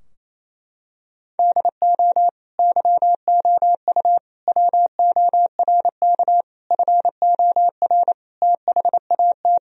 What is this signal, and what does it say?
Morse code, heard as a keyed tone, about 28 wpm, 700 Hz. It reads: DO YOU WORK FOR THAT
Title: Sentences from Top 1000 Words - Review Part 3 - 28wpm